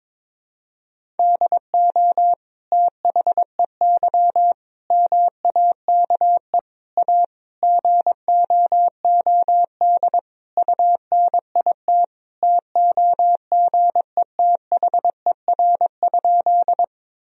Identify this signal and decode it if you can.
Morse code, heard as a keyed tone, about 22 wpm, 700 Hz. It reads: DO THEY MAKE A GOOD UNIT TOGETHER?
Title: Sentences from Top 1000 Words - Review Part 2 - 22wpm